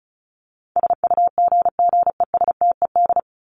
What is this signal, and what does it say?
Morse code, heard as a keyed tone, about 34 wpm, 700 Hz. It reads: SUGGESTED
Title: ED Suffix - Top 100 Words with Base Words 35wpm